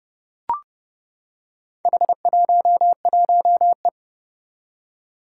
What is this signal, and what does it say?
Morse code, heard as a keyed tone, about 30 wpm, 700 Hz. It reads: H11E
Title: Four Challenging Letter-Number Combinations 30wpm